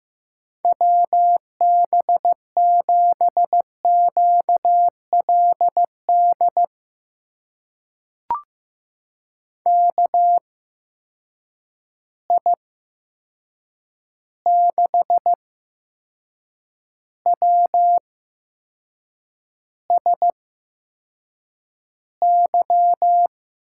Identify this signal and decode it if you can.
Morse code, heard as a keyed tone, about 15 wpm, spaced as for 6 wpm, 700 Hz. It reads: WB7QLD KI6WSY
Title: USA Call Signs - 6 Characters - 8x Character Spacing - 15wpm